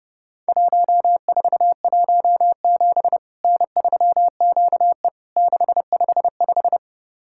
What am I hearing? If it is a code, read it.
Morse code, heard as a keyed tone, about 32 wpm, 700 Hz. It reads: 1417 N3QE 655
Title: Worked All Europe DX Contest - QTC Exchange - 30wpm